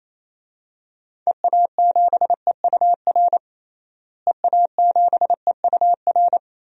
Morse code, heard as a keyed tone, about 28 wpm, 700 Hz. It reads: EA7EUR EA7EUR
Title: DXCC-100 Call Signs - 6 Characters - Repeated 2x - 28wpm